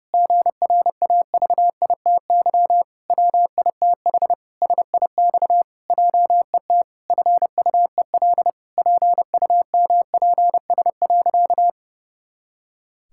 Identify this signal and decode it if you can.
Morse code, heard as a keyed tone, about 30 wpm, 700 Hz. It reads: GRAVITY WITH SIX JET FUEL PUMPS.
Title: Pangram Challenge v2 - 30wpm